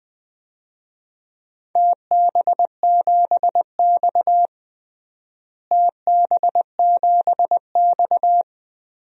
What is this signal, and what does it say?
Morse code, heard as a keyed tone, about 21 wpm, 700 Hz. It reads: TB7X TB7X
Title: DXCC-100 Call Signs - 4 Characters - Repeated 3x - 20wpm